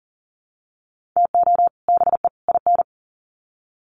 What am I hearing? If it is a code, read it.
Morse code, heard as a keyed tone, about 38 wpm, 700 Hz. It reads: TO BE IN
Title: Sets of 3 Words 40wpm